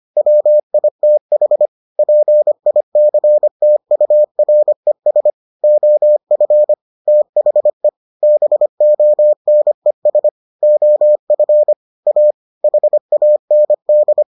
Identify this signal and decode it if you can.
Morse code, heard as a keyed tone, about 25 wpm, 600 Hz. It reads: WITH PICTURES OF THE BONES OF A HAND